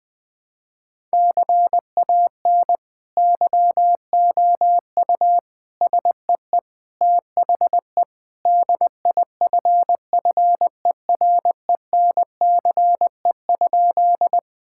Morse code, heard as a keyed tone, about 20 wpm, 700 Hz. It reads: CAN YOU SEE THE DIFFERENCE?
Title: Sentences with 2-Word Phrases - 20wpm